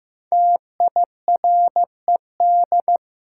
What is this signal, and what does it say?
Morse code, heard as a keyed tone, about 15 wpm, 700 Hz. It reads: TIRED
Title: Five Letter Words - 15wpm